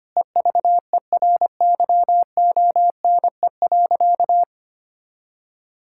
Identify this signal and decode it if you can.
Morse code, heard as a keyed tone, about 25 wpm, 700 Hz. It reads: EVERYONE.
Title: Sentences from Top 1000 Words - Review Part 6 - 25wpm